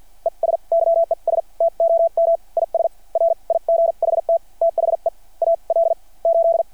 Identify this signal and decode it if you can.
Morse code, heard as a keyed tone, about 52 wpm, spaced as for 39 wpm, 600 Hz. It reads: ES YES TOM IS RIGHT THE AL8
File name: archived / CRSnet